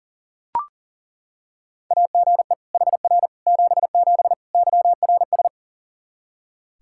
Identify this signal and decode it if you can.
Morse code, heard as a keyed tone, about 40 wpm, 700 Hz. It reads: AGE HR 77 YRS
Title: Practice QSO - Exchange 3 - More Info - 40wpm